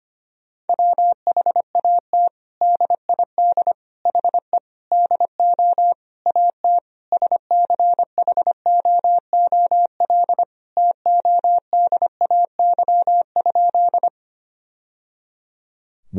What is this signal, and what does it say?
Morse code, heard as a keyed tone, about 25 wpm, 700 Hz. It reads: WHAT DID HE DO AT SCHOOL TODAY?
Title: Sentences from Top 1000 Words - Review Part 1 - 25wpm